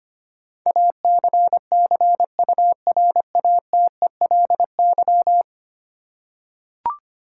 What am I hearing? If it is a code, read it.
Morse code, heard as a keyed tone, about 25 wpm, 700 Hz. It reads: ACCURATELY
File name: Ten Letter Words - 25wpm